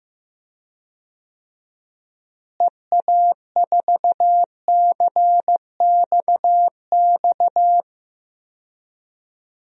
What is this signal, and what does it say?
Morse code, heard as a keyed tone, about 15 wpm, 700 Hz. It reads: EA4CXX
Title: DXCC-100 Call Signs - 6 Characters - Repeated 3x - 15wpm